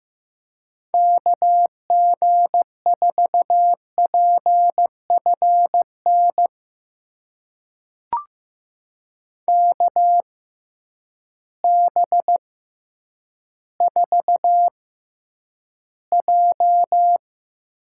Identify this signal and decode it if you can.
Morse code, heard as a keyed tone, about 15 wpm, spaced as for 9 wpm, 700 Hz. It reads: KG4PFN KB4J
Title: USA Call Signs - 6 Characters - 6x Character Spacing - 15wpm